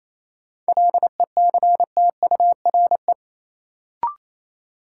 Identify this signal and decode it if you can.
Morse code, heard as a keyed tone, about 28 wpm, 700 Hz. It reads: LECTURE
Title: Sentences from Top 1600 Words - New Word Drill - 28wpm